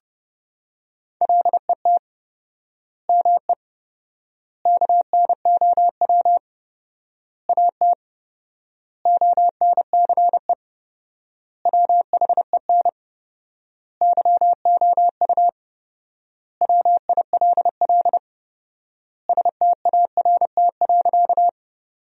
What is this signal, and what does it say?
Morse code, heard as a keyed tone, about 30 wpm, 700 Hz. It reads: LET ME KNOW AT ONCE WHEN YOU WILL START.
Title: 4x Word-Spacing - Sentences from Top 300 Words - 30wpm